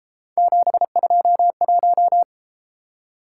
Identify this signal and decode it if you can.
Morse code, heard as a keyed tone, about 31 wpm, 700 Hz. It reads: 721